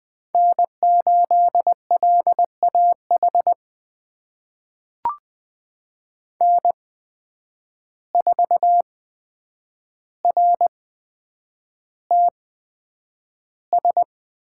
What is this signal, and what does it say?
Morse code, heard as a keyed tone, about 20 wpm, spaced as for 9 wpm, 700 Hz. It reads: N8LAH N4RTS
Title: USA Call Signs - 5 Characters - 8x Character Spacing - 20wpm